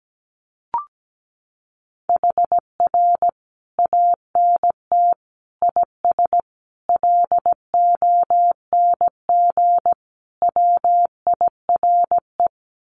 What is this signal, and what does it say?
Morse code, heard as a keyed tone, about 17 wpm, 700 Hz. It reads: HR ANT IS LONG WIRE
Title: Common QSO Phrases 17wpm